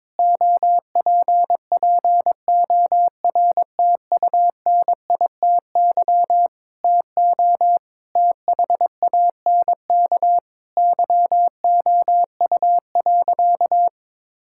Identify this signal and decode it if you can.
Morse code, heard as a keyed tone, about 22 wpm, 700 Hz. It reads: OPPORTUNITY TO THANK YOU.
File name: Sentences with 7-Word Phrases - 22wpm